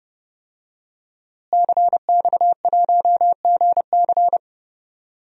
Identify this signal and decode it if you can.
Morse code, heard as a keyed tone, about 31 wpm, 700 Hz. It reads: CX1GC
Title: DXCC-100 Call Signs - 5 Characters - Repeated 3x - 30wpm